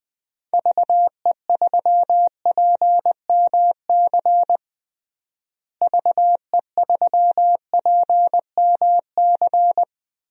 Morse code, heard as a keyed tone, about 21 wpm, 700 Hz. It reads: VE3PMC VE3PMC
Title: DXCC-100 Call Signs - 6 Characters - Repeated 3x - 20wpm